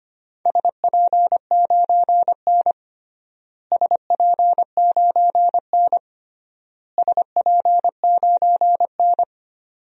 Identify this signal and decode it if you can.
Morse code, heard as a keyed tone, about 25 wpm, 700 Hz. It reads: SP9N SP9N SP9N
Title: DXCC-100 Call Signs - 4 Characters - Repeated 3x - 25wpm